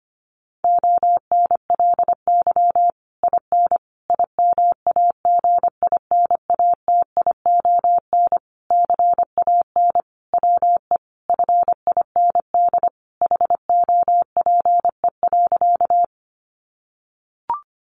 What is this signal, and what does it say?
Morse code, heard as a keyed tone, about 25 wpm, 700 Hz. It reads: ONLY IN IMAGINATION CAN WE FIND HOPE.
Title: Sentences from Top 1300 Words 25wpm